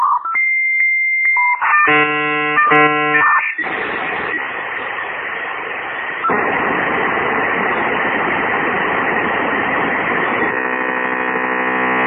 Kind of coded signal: dial-up modem handshake